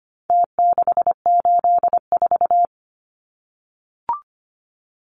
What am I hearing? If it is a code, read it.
Morse code, heard as a keyed tone, about 25 wpm, 700 Hz. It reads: T684
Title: Numbers - Contest Style - 4-Digit 25wpm